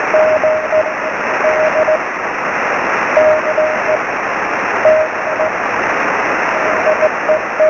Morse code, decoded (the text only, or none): none